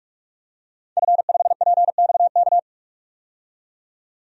Morse code, heard as a keyed tone, about 46 wpm, 700 Hz. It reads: F5PXK
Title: DXCC-100 Call Signs - 5 Characters - 45wpm